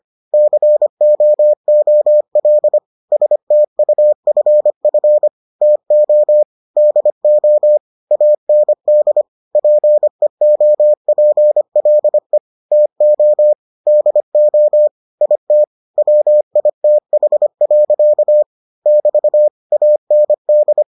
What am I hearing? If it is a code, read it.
Morse code, heard as a keyed tone, about 25 wpm, 600 Hz. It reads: COOL STUFF TO DO AND PEOPLE TO DO IT WITH. <BT> AND